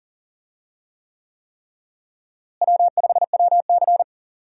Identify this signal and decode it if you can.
Morse code, heard as a keyed tone, about 40 wpm, 700 Hz. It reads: W5WC
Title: USA Call Signs - 4 Characters - Repeated 3x - 40wpm